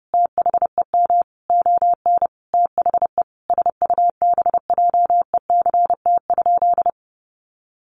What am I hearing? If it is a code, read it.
Morse code, heard as a keyed tone, about 30 wpm, 700 Hz. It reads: THEM ON THE SUBJECT?
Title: Sentences from Top 800 Words 30wpm